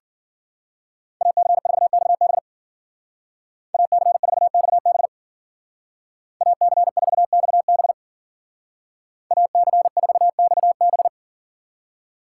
Morse code, heard as a keyed tone, about 49 wpm, 700 Hz. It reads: AC4XB AC4XB AC4XB AC4XB
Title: Speed Racing - USA Call Signs 40wpm